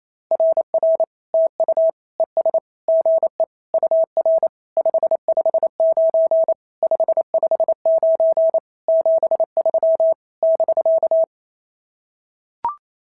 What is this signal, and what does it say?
Morse code, heard as a keyed tone, about 28 wpm, 690 Hz. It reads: RR TU ES GE UR 559 559 73 <BK>
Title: SOTA QSO Practice 28wpm